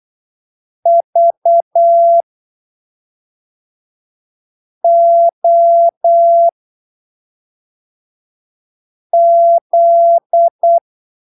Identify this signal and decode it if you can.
Morse code, heard as a keyed tone, about 8 wpm, spaced as for 3 wpm, 680 Hz. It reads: VOZ
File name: arquivos / texto / texto 13 08ppm